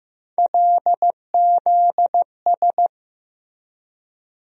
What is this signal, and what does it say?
Morse code, heard as a keyed tone, about 15 wpm, 700 Hz. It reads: LZS
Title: USA Call Sign Postfixes - 3 Letters 15wpm